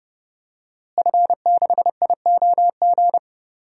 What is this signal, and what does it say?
Morse code, heard as a keyed tone, about 30 wpm, 700 Hz. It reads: F6IOG
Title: DXCC-100 Call Signs - 5 Characters - Repeated 2x - 30wpm